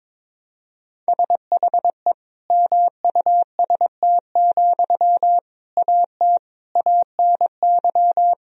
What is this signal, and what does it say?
Morse code, heard as a keyed tone, about 22 wpm, 700 Hz. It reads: SHE MUST, AT ANY
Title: Sentences from Top 1000 Words - Review Part 4 - 22wpm